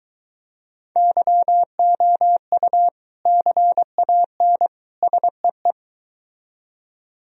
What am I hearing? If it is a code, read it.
Morse code, heard as a keyed tone, about 23 wpm, 700 Hz. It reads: YOU CAN SEE